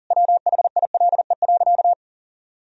Morse code, heard as a keyed tone, about 40 wpm, 700 Hz. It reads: WHILE.
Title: Sentences from Top 1000 Words - Review Part 2 - 40wpm